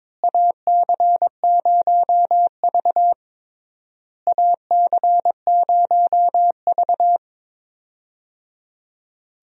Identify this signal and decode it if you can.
Morse code, heard as a keyed tone, about 22 wpm, 700 Hz. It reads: AC0V AC0V
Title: USA Call Signs - 4 Characters - Repeated 3x - 22wpm